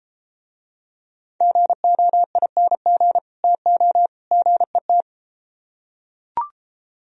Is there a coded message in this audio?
Yes: Morse code from 1.4 to 5.0 s, heard as a keyed tone, about 32 wpm, 700 Hz. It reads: GOING TO GET